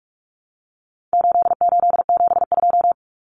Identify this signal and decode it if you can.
Morse code, heard as a keyed tone, about 44 wpm, 700 Hz. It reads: 8872
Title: Numbers - 4-Digit 45wpm